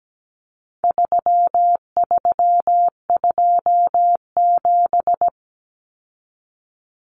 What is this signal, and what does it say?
Morse code, heard as a keyed tone, about 17 wpm, 700 Hz. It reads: 3327
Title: Common Cut Numbers - 4-Digit 17wpm